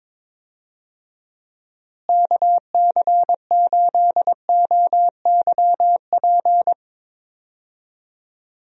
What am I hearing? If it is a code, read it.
Morse code, heard as a keyed tone, about 22 wpm, 700 Hz. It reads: KC8OYP
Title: USA Call Signs - 6 Characters - Repeated 2x - 22wpm